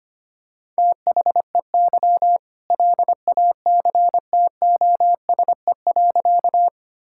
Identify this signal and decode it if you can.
Morse code, heard as a keyed tone, about 25 wpm, 700 Hz. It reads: THEY LACTOSE.
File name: practice / 25wpm / All-Age Appropriate Jokes - 25wpm